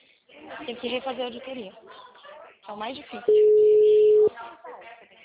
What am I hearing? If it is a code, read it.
Ringback tone (Europe, 425 Hz), with speech.
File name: out- -2025